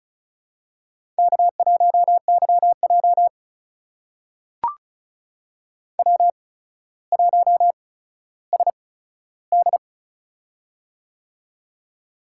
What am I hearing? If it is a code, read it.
Morse code, heard as a keyed tone, about 36 wpm, spaced as for 15 wpm, 700 Hz. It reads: K1YJ W1SD
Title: USA Call Signs - 4 Characters - 8x Character Spacing - 35wpm